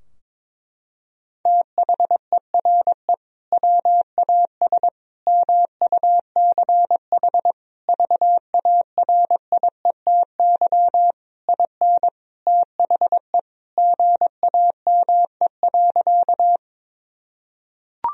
Morse code, heard as a keyed tone, about 21 wpm, 700 Hz. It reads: THERE WAS MUCH VARIETY IN THE GAME.
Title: Sentences from Top 1000 Words - Review Part 4 - 22wpm